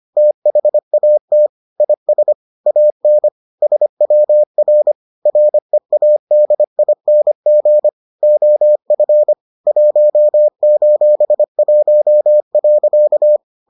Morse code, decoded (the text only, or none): THAT IS AN SWR READING OF 1:1.